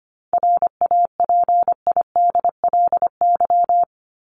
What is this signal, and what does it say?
Morse code, heard as a keyed tone, about 25 wpm, 700 Hz. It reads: RAPIDLY
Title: LY Suffix - Top 100 Words with Base Words 25wpm